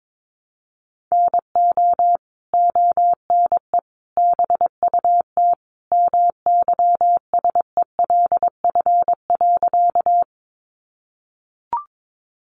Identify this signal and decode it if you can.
Morse code, heard as a keyed tone, about 22 wpm, 700 Hz. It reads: NO ONE BUT MYSELF.
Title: Sentences from Top 1300 Words 22wpm